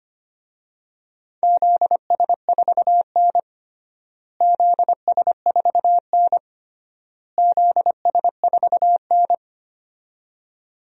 Morse code, heard as a keyed tone, about 25 wpm, 700 Hz. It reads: ZS4N ZS4N ZS4N
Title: DXCC-100 Call Signs - 4 Characters - Repeated 3x - 25wpm